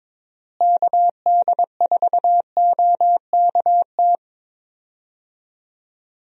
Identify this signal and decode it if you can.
Morse code, heard as a keyed tone, about 22 wpm, 700 Hz. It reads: KD4OKT
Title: USA Call Signs - 6 Characters - Repeated 3x - 22wpm